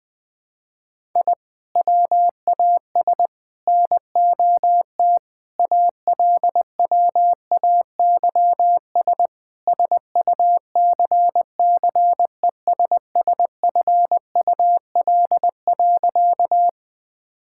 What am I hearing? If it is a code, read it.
Morse code, heard as a keyed tone, about 20 wpm, 700 Hz. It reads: I WAS NOT ALWAYS SUCCESSFUL.
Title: Sentences from Top 1000 Words - Review Part 6 - 20wpm